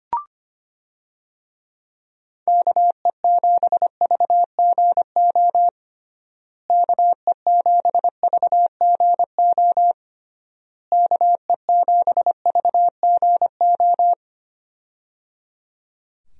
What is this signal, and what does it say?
Morse code, heard as a keyed tone, about 25 wpm, 700 Hz. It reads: KE7VGO KE7VGO KE7VGO
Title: USA Call Signs - 6 Characters - Repeated 3x - 25wpm